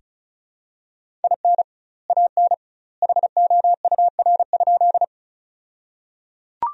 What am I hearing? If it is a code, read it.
Morse code, heard as a keyed tone, about 36 wpm, 700 Hz. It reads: IN AN HOUR?
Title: 2x Word-Spacing - Sentences from Top 400 Words - 35wpm